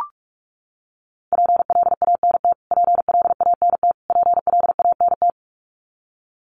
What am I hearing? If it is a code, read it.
Morse code, heard as a keyed tone, about 44 wpm, 700 Hz. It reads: PLANT PLANT PLANT
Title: Top 500 Words Repeated 45wpm